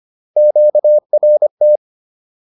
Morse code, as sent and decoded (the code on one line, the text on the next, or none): --.- .-. -
QRT